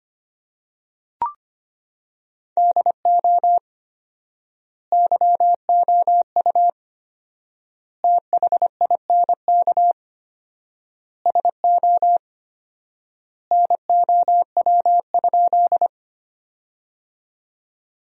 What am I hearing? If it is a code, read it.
Morse code, heard as a keyed tone, about 25 wpm, 700 Hz. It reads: DO YOU THINK SO NOW?
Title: 4x Word-Spacing - Sentences from Top 200 Words - 25wpm